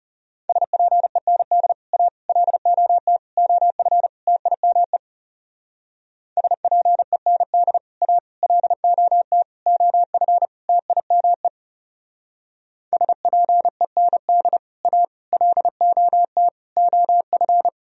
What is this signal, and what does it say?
Morse code, heard as a keyed tone, about 40 wpm, 700 Hz. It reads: SPEND A LOT OF TIME SPEND A LOT OF TIME SPEND A LOT OF